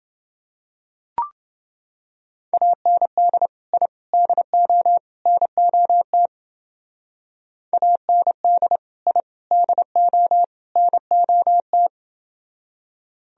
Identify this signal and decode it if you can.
Morse code, heard as a keyed tone, about 28 wpm, 700 Hz. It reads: AND I DO NOT AND I DO NOT